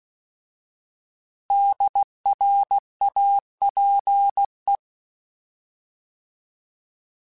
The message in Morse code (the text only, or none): DRAPE